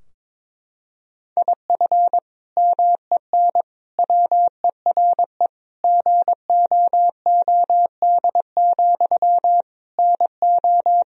Morse code, heard as a keyed tone, about 22 wpm, 700 Hz. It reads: IF MEN WERE GOOD, NO
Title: Sentences from Top 1000 Words - Review Part 3 - 22wpm